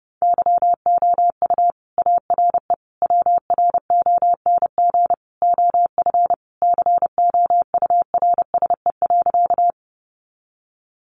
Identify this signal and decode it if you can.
Morse code, heard as a keyed tone, about 30 wpm, 700 Hz. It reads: YOU ARE WRONG OF COURSE.
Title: Sentences from Top 900 Words 30wpm